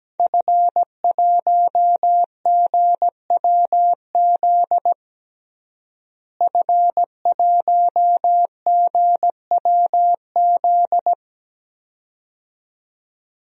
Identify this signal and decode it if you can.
Morse code, heard as a keyed tone, about 17 wpm, 700 Hz. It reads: F1GWZ F1GWZ
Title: DXCC-100 Call Signs - 5 Characters - Repeated 2x - 17wpm